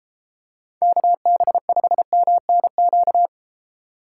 Morse code, heard as a keyed tone, about 32 wpm, 700 Hz. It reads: KB5MNQ